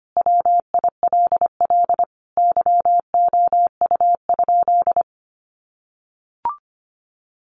Morse code, heard as a keyed tone, about 25 wpm, 700 Hz. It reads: WILL YOU?
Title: Sentences from Top 300 Words 25wpm